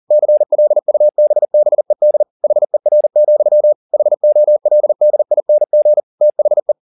Morse code, heard as a keyed tone, about 40 wpm, 600 Hz. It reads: CLUBBED HER, HOLDING THE